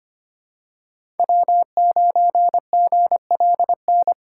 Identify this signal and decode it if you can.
Morse code, heard as a keyed tone, about 25 wpm, 700 Hz. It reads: W9GLN
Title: USA Call Signs - 5 Characters - 3x Character Spacing - 25wpm